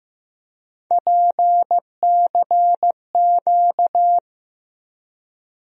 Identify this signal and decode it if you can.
Morse code, heard as a keyed tone, about 15 wpm, 700 Hz. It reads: PCQ
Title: USA Call Sign Postfixes - 3 Letters 15wpm